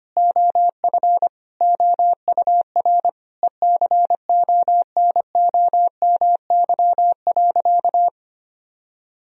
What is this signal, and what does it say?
Morse code, heard as a keyed tone, about 25 wpm, 700 Hz. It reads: OF OUR ECONOMY.
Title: Sentences from Top 1000 Words - Review Part 3 - 25wpm